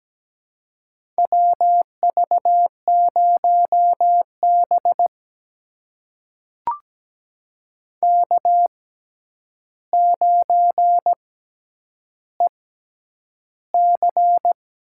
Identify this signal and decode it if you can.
Morse code, heard as a keyed tone, about 17 wpm, spaced as for 9 wpm, 700 Hz. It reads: WV0B K9EC
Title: USA Call Signs - 4 Characters - 6x Character Spacing - 17wpm